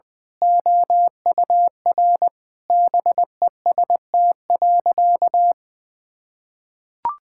Morse code, heard as a keyed tone, about 20 wpm, 700 Hz. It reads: OUR BEST.
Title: Sentences with 2-Word Phrases - 20wpm